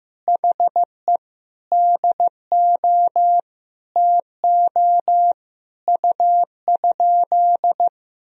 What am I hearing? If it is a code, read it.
Morse code, heard as a keyed tone, about 15 wpm, 700 Hz. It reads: HE DO TO U?
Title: Sentences from Top 100 Words and Abbreviations 15wpm